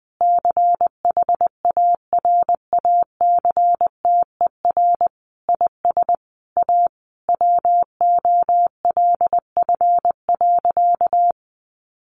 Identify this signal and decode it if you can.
Morse code, heard as a keyed tone, about 20 wpm, 700 Hz. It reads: CHARACTER IS A WOLF.
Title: Additional Easy Phrases 20wpm